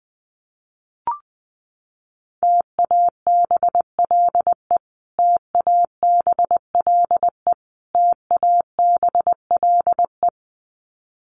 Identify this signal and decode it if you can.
Morse code, heard as a keyed tone, about 20 wpm, 700 Hz. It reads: TABLE TABLE TABLE
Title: Top 500 Words Repeated 20wpm